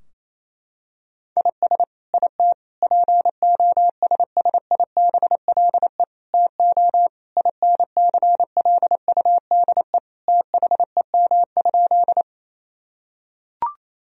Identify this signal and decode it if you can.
Morse code, heard as a keyed tone, about 28 wpm, 700 Hz. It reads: IS IT POSSIBLE TO INCLUDE THEM?
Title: Sentences from Top 1000 Words - Review Part 3 - 28wpm